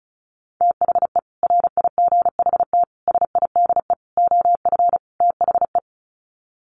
Morse code, heard as a keyed tone, about 34 wpm, 700 Hz. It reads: THE RIGHT SIDE OF THE
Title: Sets of 5 Words Encore 35wpm